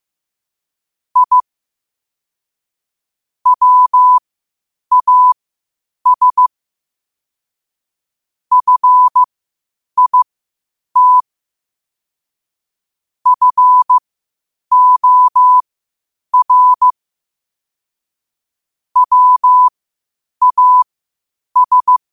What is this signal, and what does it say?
Morse code, heard as a keyed tone, about 15 wpm, spaced as for 8 wpm, 1000 Hz. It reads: I WAS FIT FOR WAS
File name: qotdfiles / 8wpm